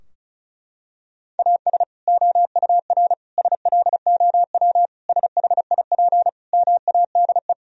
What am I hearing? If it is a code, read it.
Morse code, heard as a keyed tone, about 34 wpm, 700 Hz. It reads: AS OUR SLOW SHIP MADE
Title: Sentences from Top 1000 Words - Review Part 2 - 35wpm